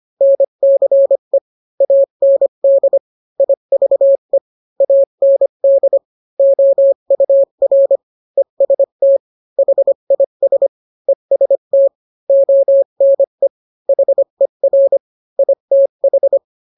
Morse code, heard as a keyed tone, about 25 wpm, 550 Hz. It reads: NCE AND IVE AND OUR EST HIS EST ONE HER ITH